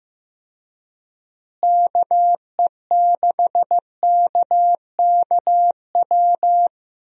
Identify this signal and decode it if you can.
Morse code, heard as a keyed tone, about 15 wpm, 700 Hz. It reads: KE6KKW
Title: USA Call Signs - 6 Characters - Repeated 3x - 15wpm